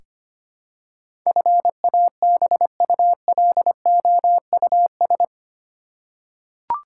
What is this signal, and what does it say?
Morse code, heard as a keyed tone, about 25 wpm, 700 Hz. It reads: FABULOUS
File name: 8 Letter Words - First 3 Letters Given - 25wpm